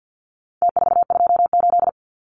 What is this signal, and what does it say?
Morse code, heard as a keyed tone, about 50 wpm, 700 Hz. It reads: T428
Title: Numbers - Contest Style - 4-Digit 50wpm